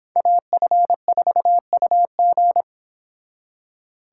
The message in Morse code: AF4UG

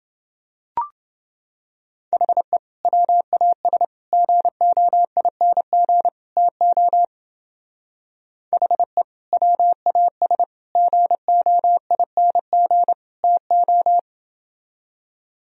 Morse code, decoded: HE WAS GOING TO HE WAS GOING TO

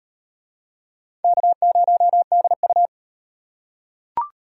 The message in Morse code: K0DU

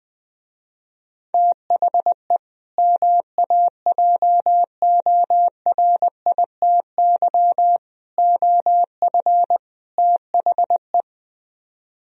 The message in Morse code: THE MAJORITY OF THE

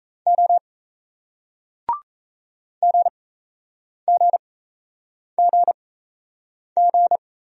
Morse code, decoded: O GGGG

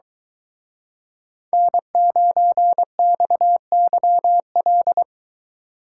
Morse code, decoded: N9XYL

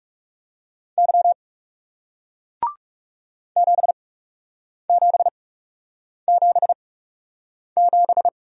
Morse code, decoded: Y 7777

